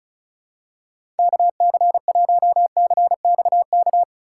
Morse code, heard as a keyed tone, about 36 wpm, 700 Hz. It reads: KC1CXK
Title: USA Call Signs - 6 Characters - 4x Character Spacing - 35wpm